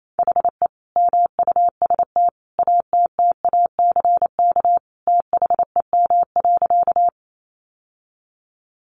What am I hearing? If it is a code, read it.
Morse code, heard as a keyed tone, about 28 wpm, 700 Hz. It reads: HE MUST ATTACK THEM.
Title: Sentences from Top 1500 Words 28wpm